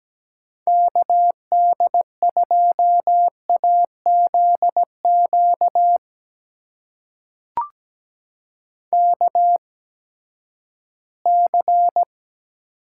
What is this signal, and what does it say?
Morse code, heard as a keyed tone, about 17 wpm, spaced as for 12 wpm, 700 Hz. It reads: KD2AZQ KC
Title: USA Call Signs - 6 Characters - 8x Character Spacing - 17wpm